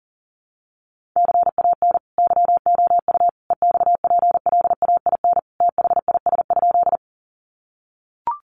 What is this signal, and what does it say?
Morse code, heard as a keyed tone, about 40 wpm, 700 Hz. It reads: CAN YOU EXPLAIN THIS?
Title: Sentences from Top 1100 Words 40wpm